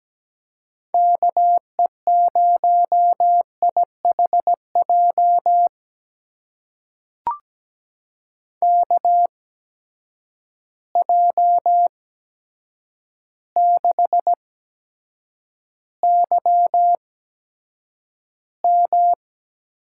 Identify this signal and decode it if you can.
Morse code, heard as a keyed tone, about 17 wpm, spaced as for 8 wpm, 700 Hz. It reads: KE0IHJ KJ6YM
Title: USA Call Signs - 6 Characters - 8x Character Spacing - 17wpm